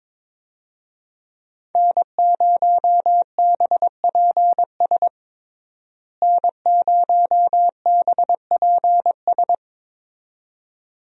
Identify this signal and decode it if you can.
Morse code, heard as a keyed tone, about 22 wpm, 700 Hz. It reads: N0BPS N0BPS
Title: USA Call Signs - 5 Characters - Repeated 2x - 22wpm